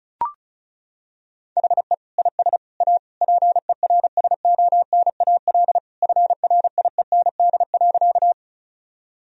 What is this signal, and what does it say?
Morse code, heard as a keyed tone, about 36 wpm, 700 Hz. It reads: HE IS A PERSONAL FRIEND.
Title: Sentences from Top 1000 Words - Review Part 4 - 35wpm